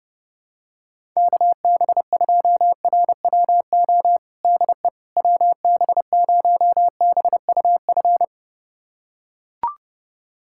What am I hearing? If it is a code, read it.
Morse code, heard as a keyed tone, about 30 wpm, 700 Hz. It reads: KB2RWO DE WB0BUF
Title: USA Callsign DE Callsign - 6 Characters - 30wpm